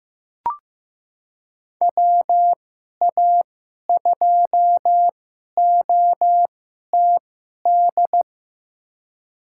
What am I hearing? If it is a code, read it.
Morse code, heard as a keyed tone, about 15 wpm, spaced as for 11 wpm, 700 Hz. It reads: WA2OTD
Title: USA Call Signs - 6 Characters - 2x Character Spacing - 15wpm